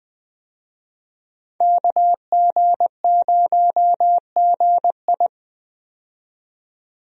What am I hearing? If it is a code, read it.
Morse code, heard as a keyed tone, about 20 wpm, 700 Hz. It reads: KG0GI
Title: USA Call Signs - 5 Characters - 20wpm